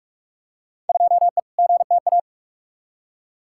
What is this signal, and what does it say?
Morse code, heard as a keyed tone, about 46 wpm, 700 Hz. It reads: 2E GTA
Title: Field Day Exchanges Encore - 45wpm